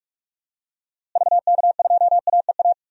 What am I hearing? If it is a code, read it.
Morse code, heard as a keyed tone, about 44 wpm, 700 Hz. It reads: VK2AEA
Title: DXCC-100 Call Signs - 6 Characters - Repeated 2x - 45wpm